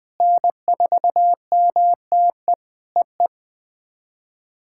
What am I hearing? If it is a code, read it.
Morse code, heard as a keyed tone, about 21 wpm, 700 Hz. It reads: N4MTE EE
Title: POTA QSO Practice - 20wpm